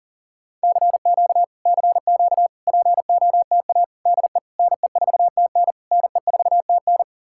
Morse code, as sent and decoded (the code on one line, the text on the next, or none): -.-. --.- / -.-. --.- / .--. --- - .- / -.. / -. ....- - -. / -. ....- - -.
CQ CQ POTA DE NE4TN NE4TN